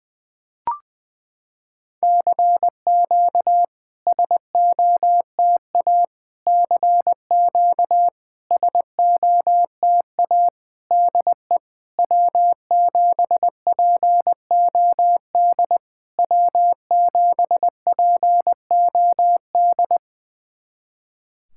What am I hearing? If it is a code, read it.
Morse code, heard as a keyed tone, about 20 wpm, 700 Hz. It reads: CQ SOTA CQ SOTA DE W7POD W7POD
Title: SOTA QSO Practice 20wpm